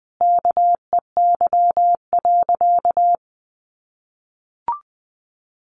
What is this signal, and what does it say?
Morse code, heard as a keyed tone, about 20 wpm, 700 Hz. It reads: KEY.
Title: Sentences from Top 1300 Words 20wpm